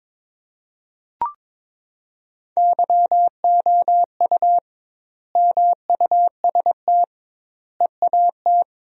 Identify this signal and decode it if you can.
Morse code, heard as a keyed tone, about 22 wpm, 700 Hz. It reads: YOU MUST EAT
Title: 2x Word-Spacing - Sentences from Top 300 Words - 22wpm